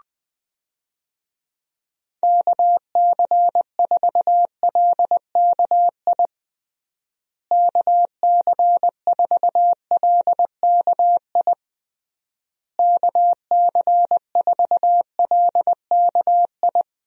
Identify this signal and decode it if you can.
Morse code, heard as a keyed tone, about 20 wpm, 700 Hz. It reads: KC4LKI KC4LKI KC4LKI
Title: USA Call Signs - 6 Characters - Repeated 3x - 20wpm